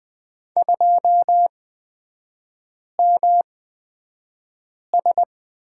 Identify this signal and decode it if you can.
Morse code, heard as a keyed tone, about 20 wpm, spaced as for 5 wpm, 700 Hz. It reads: 2MS
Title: 250 Lesson 40e - BK - Three Characters 20wpm